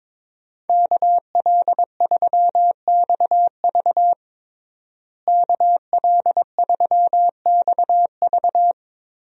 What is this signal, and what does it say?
Morse code, heard as a keyed tone, about 22 wpm, 700 Hz. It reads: KL3XV KL3XV
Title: USA Call Signs - 5 Characters - Repeated 3x - 22wpm